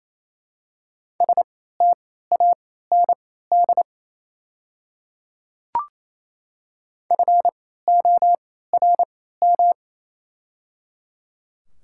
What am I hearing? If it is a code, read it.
Morse code, heard as a keyed tone, about 28 wpm, spaced as for 16 wpm, 700 Hz. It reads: STAND FORM
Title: 3x Character Spacing - Top 500 Words - 28wpm